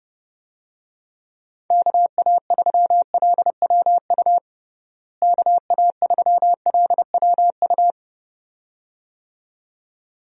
Morse code, heard as a keyed tone, about 32 wpm, 700 Hz. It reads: KA3LWU KA3LWU
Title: USA Call Signs - 6 Characters - Repeated 2x - 30wpm